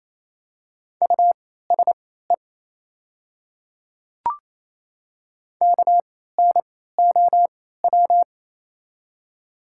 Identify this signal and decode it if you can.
Morse code, heard as a keyed tone, about 28 wpm, spaced as for 16 wpm, 700 Hz. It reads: USE KNOW
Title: 3x Character Spacing - Top 200 Words - 28wpm